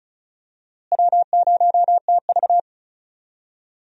Morse code, heard as a keyed tone, about 34 wpm, 700 Hz. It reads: W0TV
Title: USA Call Signs - 4 Characters - Repeated 3x - 35wpm